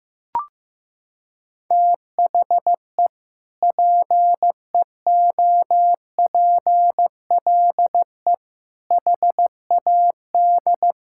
Morse code, heard as a keyed tone, about 15 wpm, 700 Hz. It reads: THE PEOPLE HAD
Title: Sentences from Top 1000 Words - Review Part 6 - 15wpm